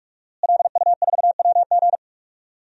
Morse code, heard as a keyed tone, about 46 wpm, 700 Hz. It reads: LU4WG
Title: DXCC-100 Call Signs - 5 Characters - Repeated 3x - 45wpm